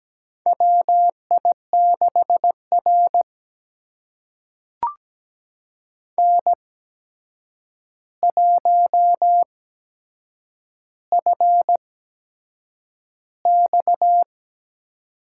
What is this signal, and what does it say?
Morse code, heard as a keyed tone, about 17 wpm, spaced as for 7 wpm, 700 Hz. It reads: WI6R N1FX
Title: USA Call Signs - 4 Characters - 8x Character Spacing - 17wpm